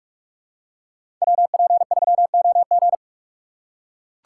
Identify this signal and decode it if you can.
Morse code, heard as a keyed tone, about 44 wpm, 690 Hz. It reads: WP3OG
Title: USA Call Signs - 6x Character Spacing - 45wpm